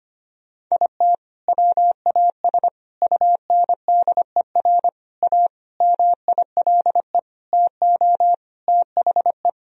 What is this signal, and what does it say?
Morse code, heard as a keyed tone, about 25 wpm, 700 Hz. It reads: IT WAS UNDER A MILE TO THE
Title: Sentences from Top 1000 Words - Review Part 1 - 25wpm